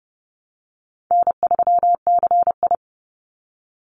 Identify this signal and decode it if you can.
Morse code, heard as a keyed tone, about 30 wpm, 700 Hz. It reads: N3CI
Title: Easy Call Signs - 30wpm